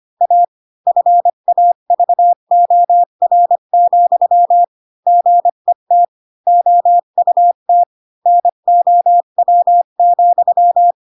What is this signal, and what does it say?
Morse code, heard as a keyed tone, about 25 wpm, 700 Hz. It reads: A FAVOR, GET OUT NOW,
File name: Pinocchio / Pinocchio ch4 20 25 ik1hge